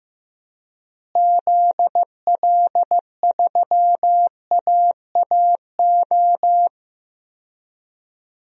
Morse code, heard as a keyed tone, about 15 wpm, 700 Hz. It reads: ZL3AAO
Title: DXCC-100 Call Signs - 6 Characters - Repeated 2x - 15wpm